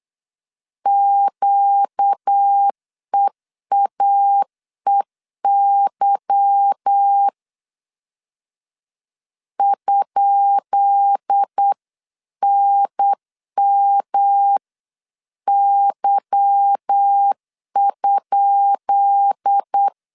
Morse code, heard as a keyed tone, about 8 wpm, 800 Hz. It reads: QEAEY ?NMY?
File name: LEZIONI / 10 SIGNS / 10B INT 8wpm 3534